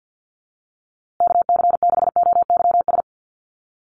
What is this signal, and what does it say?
Morse code, heard as a keyed tone, about 48 wpm, 700 Hz. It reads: KC6OYS
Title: USA Call Signs Repeated 2x 50wpm